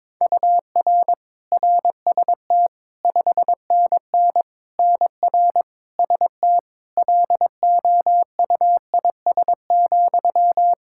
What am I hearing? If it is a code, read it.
Morse code, heard as a keyed tone, about 22 wpm, 700 Hz. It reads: UR RST 5NN NR ST LOUIS,
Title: Practice QSO - Exchange 1 - Beginning - 22wpm